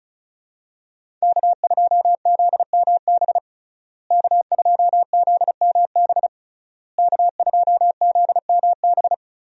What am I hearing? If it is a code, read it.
Morse code, heard as a keyed tone, about 36 wpm, 700 Hz. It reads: K2ZMB K2ZMB K2ZMB
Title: USA Call Signs - 5 Characters - Repeated 3x - 35wpm